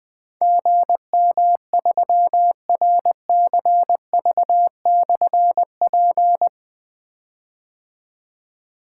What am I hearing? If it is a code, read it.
Morse code, heard as a keyed tone, about 20 wpm, 700 Hz. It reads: GM3RCV/P
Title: Difficult Call Signs 20wpm